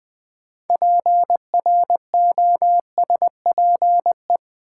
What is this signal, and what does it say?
Morse code, heard as a keyed tone, about 20 wpm, 700 Hz. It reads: PROSPE
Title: Build A Word - 2 Letters At A Time - 8-Letter Words - 20wpm